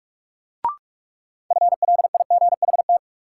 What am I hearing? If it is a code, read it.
Morse code, heard as a keyed tone, about 46 wpm, 700 Hz. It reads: FLIGHT
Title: Sentences from Top 1000 Words - New Word Drill - 45wpm